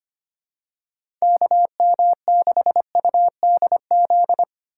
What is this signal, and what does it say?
Morse code, heard as a keyed tone, about 25 wpm, 700 Hz. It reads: KM6UDZ
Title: USA Call Signs - 6 Characters - 4x Character Spacing - 25wpm